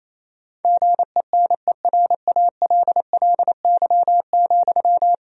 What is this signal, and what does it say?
Morse code, heard as a keyed tone, about 28 wpm, 700 Hz. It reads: GENERALLY,
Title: Sentences from Top 1000 Words - Review Part 4 - 28wpm